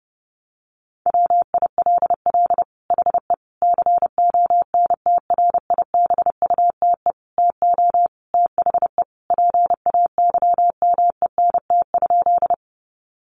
Morse code, heard as a keyed tone, about 30 wpm, 700 Hz. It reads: WILL HE CONTRIBUTE TO THE PAYMENT?
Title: Sentences from Top 1400 Words 30wpm